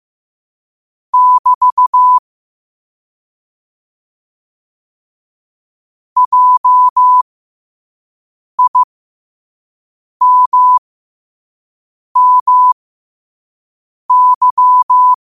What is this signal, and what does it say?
Morse code, heard as a keyed tone, about 15 wpm, spaced as for 5 wpm, 1000 Hz. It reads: <BT> JIMMY
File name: 5wpm